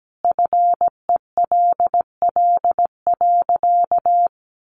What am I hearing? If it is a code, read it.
Morse code, heard as a keyed tone, about 17 wpm, 700 Hz. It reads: FELL.
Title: A Sound Of Thunder 17wpm